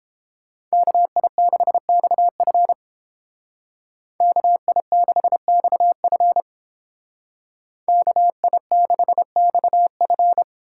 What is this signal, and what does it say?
Morse code, heard as a keyed tone, about 29 wpm, 700 Hz. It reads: KI6XF KI6XF KI6XF